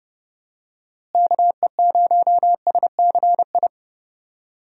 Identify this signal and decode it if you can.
Morse code, heard as a keyed tone, about 30 wpm, 700 Hz. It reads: KE0SCI
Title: USA Call Signs - 6 Characters - Repeated 2x - 30wpm